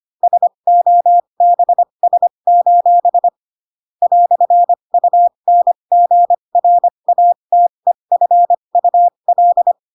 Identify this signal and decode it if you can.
Morse code, heard as a keyed tone, about 25 wpm, spaced as for 22 wpm, 700 Hz. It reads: SOBS: "UNGRATEFUL
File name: Pinocchio / Pinocchio ch3 18 25 ik1hge